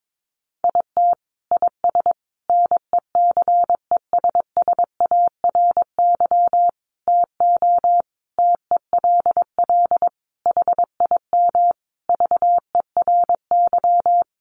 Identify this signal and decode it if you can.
Morse code, heard as a keyed tone, about 22 wpm, 700 Hz. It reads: IT IS NECESSARY TO TELL HIM VERY
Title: Sentences from Top 800 Words 22wpm